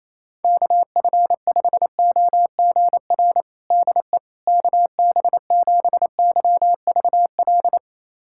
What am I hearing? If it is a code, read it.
Morse code, heard as a keyed tone, about 28 wpm, 700 Hz. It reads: KF5OGR DE KB7YVL
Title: USA Callsign DE Callsign - 6 Characters - 28wpm